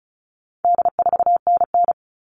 Morse code, heard as a keyed tone, about 34 wpm, 700 Hz. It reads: D4NN
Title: All Cut Numbers - 4-Digit 35wpm